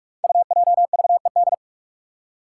Morse code, heard as a keyed tone, about 44 wpm, 700 Hz. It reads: UJVED
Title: Five Challenging Letter-Number Combinations 45wpm